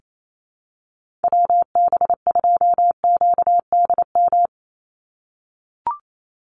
Morse code, heard as a keyed tone, about 28 wpm, 700 Hz. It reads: WB2QDM
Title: USA Call Signs 28wpm